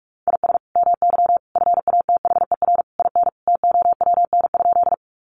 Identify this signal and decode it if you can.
Morse code, heard as a keyed tone, about 44 wpm, 700 Hz. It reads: IS MY FATHER IN TOWN?
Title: Sentences from Top 500 Words 45wpm